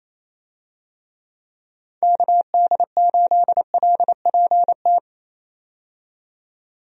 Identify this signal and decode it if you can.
Morse code, heard as a keyed tone, about 28 wpm, 700 Hz. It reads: KD8LPT
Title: USA Call Signs - 6 Characters - 28wpm